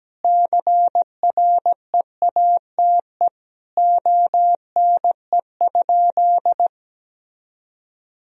Morse code, heard as a keyed tone, about 17 wpm, 700 Hz. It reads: CREATE ONE?
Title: Sentences from Top 1000 Words - Review Part 3 - 17wpm